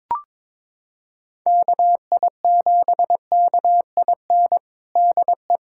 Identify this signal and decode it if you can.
Morse code, heard as a keyed tone, about 22 wpm, 700 Hz. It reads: KI7KIN DE
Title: USA Callsign DE Callsign 22wpm